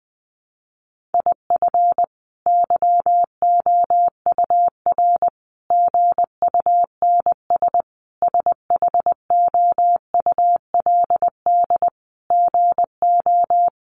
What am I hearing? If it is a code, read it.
Morse code, heard as a keyed tone, about 20 wpm, 700 Hz. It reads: IF YOUR GUNS SHOULD GO
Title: A Sound Of Thunder 20wpm